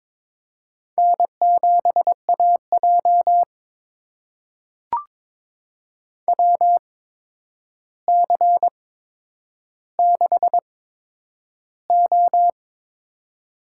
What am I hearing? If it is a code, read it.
Morse code, heard as a keyed tone, about 22 wpm, spaced as for 9 wpm, 700 Hz. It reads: N7AJ WC6O
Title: USA Call Signs - 4 Characters - 8x Character Spacing - 22wpm